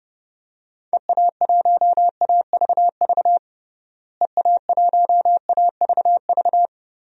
Morse code, heard as a keyed tone, about 32 wpm, 700 Hz. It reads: EA1AVV EA1AVV
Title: DXCC-100 Call Signs - 6 Characters - Repeated 3x - 30wpm